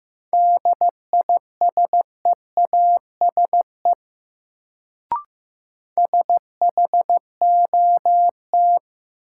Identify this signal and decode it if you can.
Morse code, heard as a keyed tone, about 16 wpm, 700 Hz. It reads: DISEASE SHOT
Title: Sentences from Top 1100 Words - New Word Drill Alt - 15wpm